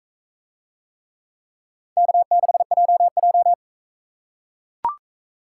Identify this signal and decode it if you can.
Morse code, heard as a keyed tone, about 41 wpm, 700 Hz. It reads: K6JJ